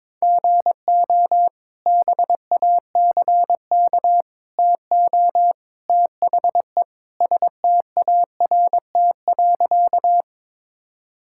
Morse code, heard as a keyed tone, about 22 wpm, 700 Hz. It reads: GO BACK TO THE START.
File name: practice / 22wpm / Sentences with 3-Word Phrases - 22wpm